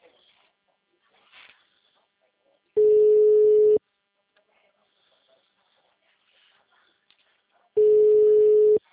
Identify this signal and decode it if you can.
Ringback tone (Europe, 425 Hz), with speech.